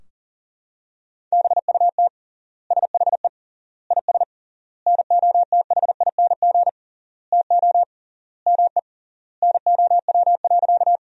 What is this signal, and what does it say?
Morse code, heard as a keyed tone, about 37 wpm, 700 Hz. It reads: BUT SHE IS NOTHING TO ME NOW.
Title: 3x Word-Spacing - Sentences from Top 500 Words - 40wpm